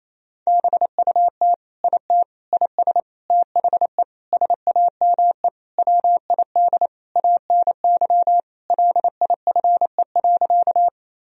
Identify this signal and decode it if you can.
Morse code, heard as a keyed tone, about 28 wpm, 700 Hz. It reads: BUT IT IS THE SAME WID ANY LIFE.
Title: Sentences from Top 300 Words and Abbreviations 28wpm